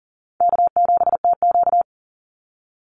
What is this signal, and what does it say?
Morse code, heard as a keyed tone, about 37 wpm, 700 Hz. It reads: K7TQ
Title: NA SOTA Chaser Callsigns 40wpm